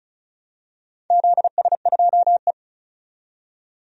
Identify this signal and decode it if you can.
Morse code, heard as a keyed tone, about 34 wpm, 700 Hz. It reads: ZS2E
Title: DXCC-100 Call Signs - 4 Characters - 35wpm